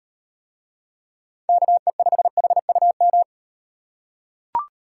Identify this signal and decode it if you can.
Morse code, heard as a keyed tone, about 38 wpm, 700 Hz. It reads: KE5HUM